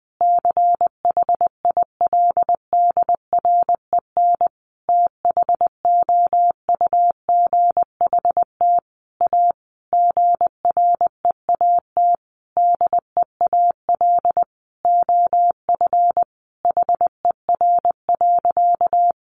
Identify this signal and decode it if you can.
Morse code, heard as a keyed tone, about 20 wpm, 700 Hz. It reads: CHILDREN THOUGHT A GREAT DEAL OF HER.
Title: Sentences from Top 800 Words 20wpm